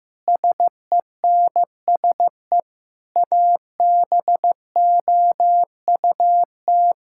Morse code, heard as a keyed tone, about 15 wpm, 700 Hz. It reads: SENSE ABOUT
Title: Sentences from Top 1000 Words - Review Part 5 - 15wpm